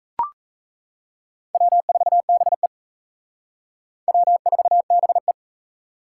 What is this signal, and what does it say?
Morse code, heard as a keyed tone, about 34 wpm, 700 Hz. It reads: W4BE W4BE
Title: Speed Racing - USA 1x2 Call Signs 28wpm